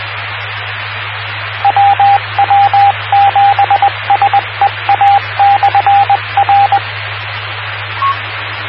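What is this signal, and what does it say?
Morse code, heard as a keyed tone, about 20 wpm, 800 Hz, with radio noise. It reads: WW7SEA/R
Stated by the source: WW7PSR repeater ident on 146.960 MHz, PSRG.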